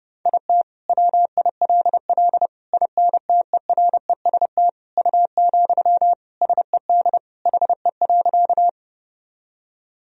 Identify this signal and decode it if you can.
Morse code, heard as a keyed tone, about 32 wpm, 700 Hz. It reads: IT WILL INTEREST U, SED HE.
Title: Sentences from Top 400 Words and Abbreviations 30wpm